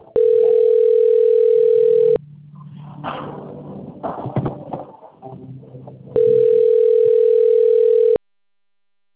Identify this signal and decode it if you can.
Ringback tone (North America), 440 + 480 Hz.